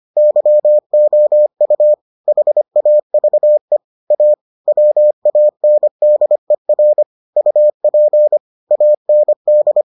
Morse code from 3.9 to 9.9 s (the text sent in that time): A WANDER UP AND